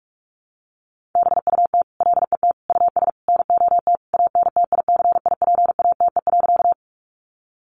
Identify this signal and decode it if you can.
Morse code, heard as a keyed tone, about 44 wpm, 700 Hz. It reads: BUT LET US NOT ANTICIPATE.
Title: Sentences from Top 1500 Words 45wpm